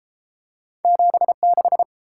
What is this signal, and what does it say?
Morse code, heard as a keyed tone, about 32 wpm, 700 Hz. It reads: 76